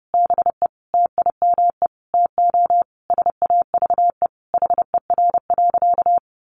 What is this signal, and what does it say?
Morse code, heard as a keyed tone, about 30 wpm, 700 Hz. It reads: BE TIME TO SAVE HER.
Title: Sentences from Top 1100 Words 30wpm